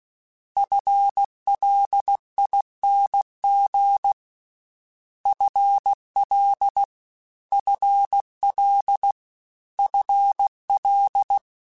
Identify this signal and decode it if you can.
Morse code, heard as a keyed tone, about 16 wpm, 800 Hz. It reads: FLING FL FL FL
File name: FL / FL mp3s / FL Prefixes 3x2 @15